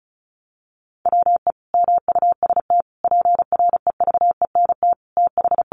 Morse code, heard as a keyed tone, about 34 wpm, 700 Hz. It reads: WE MUST PREVENT TH
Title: Sentences from Top 1100 Words 35wpm